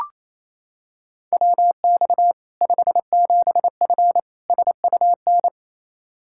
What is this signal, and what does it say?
Morse code, heard as a keyed tone, about 28 wpm, 700 Hz. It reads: WX 57F SUN
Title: K1USN SST QSO Practice 28wpm